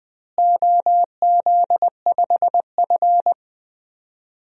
Morse code, heard as a keyed tone, about 20 wpm, 700 Hz. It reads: OZ5F
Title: DXCC-100 Call Signs - 4 Characters - Repeated 3x - 20wpm